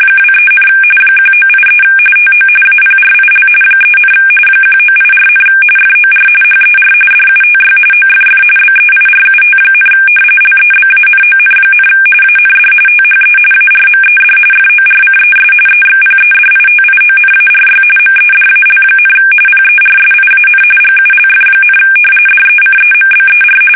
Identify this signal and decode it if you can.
RTTY, about 45.45 baud. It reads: 23, 1983 TO APRIL 3, 1984, FOR HER DUTY OFF BEIRUT, LEBANON, AND THE ARMED FORCES EXPEDITIONARY MEDAL, PERSIAN GULF, DECEMBER
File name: fsk850 / battleshipnj